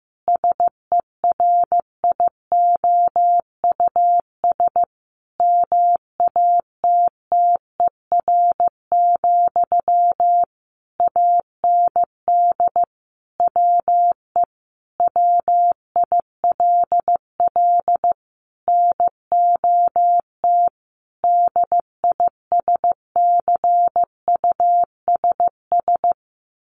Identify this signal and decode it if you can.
Morse code, heard as a keyed tone, about 15 wpm, 700 Hz. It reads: SERIOUS MATTER, AND WE WILL NOT DISCUSS
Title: Sentences from Top 1100 Words 15wpm